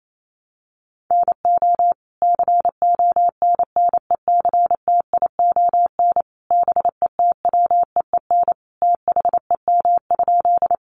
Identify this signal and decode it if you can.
Morse code, heard as a keyed tone, about 28 wpm, 700 Hz. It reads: NO CONNECTION BETWEEN THEM?
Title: Sentences from Top 1300 Words 28wpm